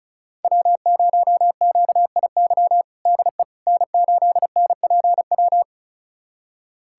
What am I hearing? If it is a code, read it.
Morse code, heard as a keyed tone, about 36 wpm, 700 Hz. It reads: W0QIY DE N8NPW
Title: USA Callsign DE Callsign - 5 Characters - 35wpm